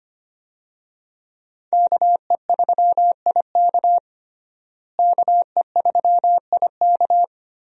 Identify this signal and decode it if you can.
Morse code, heard as a keyed tone, about 25 wpm, 700 Hz. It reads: KE3IK KE3IK
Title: USA Call Signs - 5 Characters - Repeated 3x - 25wpm